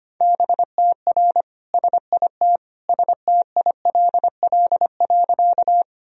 Morse code, heard as a keyed tone, about 25 wpm, 700 Hz. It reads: BTR SIT STILL.
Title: Sentences from Top 500 Words and Abbreviations 25wpm